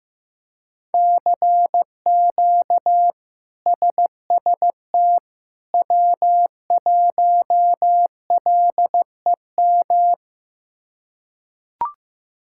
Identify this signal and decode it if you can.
Morse code, heard as a keyed tone, about 15 wpm, 700 Hz. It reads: CQ SST W1LEM
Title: K1USN SST QSO Practice - 15wpm